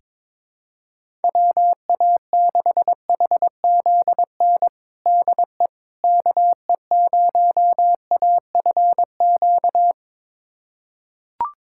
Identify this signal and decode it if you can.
Morse code, heard as a keyed tone, about 22 wpm, 700 Hz. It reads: WA6HZN DE KE0AFQ
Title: USA Callsign DE Callsign - 6 Characters - 22wpm